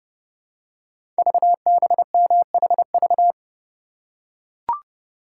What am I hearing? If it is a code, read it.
Morse code, heard as a keyed tone, about 30 wpm, 700 Hz. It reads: VBMHV
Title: Code Groups - 5 Letters - 30wpm